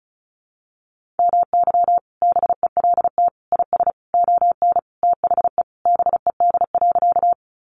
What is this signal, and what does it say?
Morse code, heard as a keyed tone, about 34 wpm, 700 Hz. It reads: MY BELT IS ON THE BED.
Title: Sentences from Top 1400 Words 35wpm